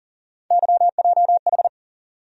Morse code, heard as a keyed tone, about 40 wpm, 700 Hz. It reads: YJH
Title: USA Call Sign Postfixes - 3 Letters 40wpm